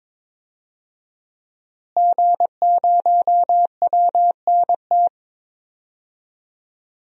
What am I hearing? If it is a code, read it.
Morse code, heard as a keyed tone, about 22 wpm, 700 Hz. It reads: G0WNT
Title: DXCC-100 Call Signs - 5 Characters - Repeated 2x - 22wpm